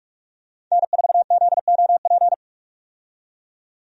Morse code, heard as a keyed tone, about 46 wpm, 700 Hz. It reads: N4ZOP
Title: USA Call Signs - 5 Characters - 45wpm